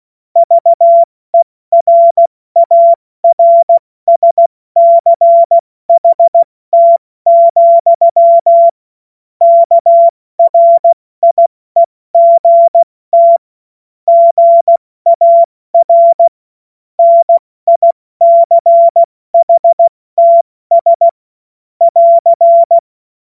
Morse code, heard as a keyed tone, about 16 wpm, 680 Hz. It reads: VERARSCHT, KRIEGT GAR NICHTS <AR>
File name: Morsekurs / Witze / Witz 09. November 2021, 70 F80